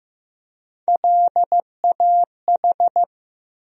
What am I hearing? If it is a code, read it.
Morse code, heard as a keyed tone, about 15 wpm, 700 Hz. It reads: LAH
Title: USA Call Sign Postfixes - 3 Letters 15wpm